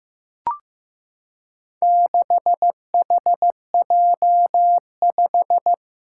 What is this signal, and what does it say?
Morse code, heard as a keyed tone, about 15 wpm, 700 Hz. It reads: 6HJ5
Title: Four Challenging Letter-Number Combinations 15wpm